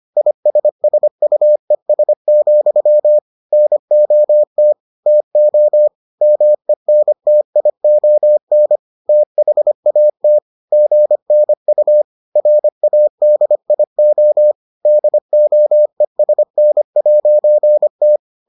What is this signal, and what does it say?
Morse code, heard as a keyed tone, about 25 wpm, 600 Hz. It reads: ISSUES, NOT TO MENTION THAT GNU RADIO DOESN'T